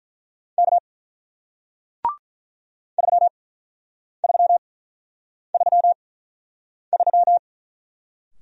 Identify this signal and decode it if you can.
Morse code, heard as a keyed tone, about 48 wpm, spaced as for 9 wpm, 700 Hz. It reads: K 3333